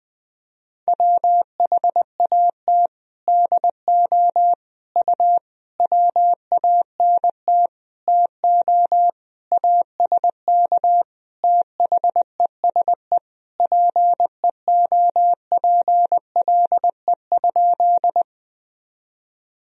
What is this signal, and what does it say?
Morse code, heard as a keyed tone, about 20 wpm, 700 Hz. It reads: WHAT DO U WANT TO ASK THESE PEOPLE?
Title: Sentences from Top 200 Words and Abbreviations 20wpm